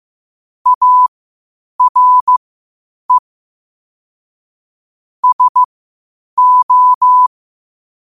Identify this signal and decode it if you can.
Morse code, heard as a keyed tone, about 15 wpm, spaced as for 8 wpm, 1000 Hz. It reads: ARE SO
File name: qotdfiles / 8wpm